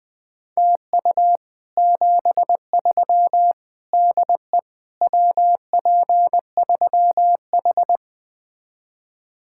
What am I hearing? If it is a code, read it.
Morse code, heard as a keyed tone, about 20 wpm, 700 Hz. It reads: TU 73 DE WP3H